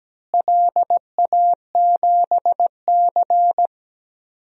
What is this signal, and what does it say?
Morse code, heard as a keyed tone, about 17 wpm, 700 Hz. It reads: LA7C
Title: DXCC-100 Call Signs - 4 Characters - Repeated 3x - 17wpm